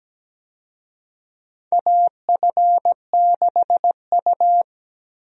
Morse code, heard as a keyed tone, about 17 wpm, 700 Hz. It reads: AF6U
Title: USA Call Signs - 4 Characters - 17wpm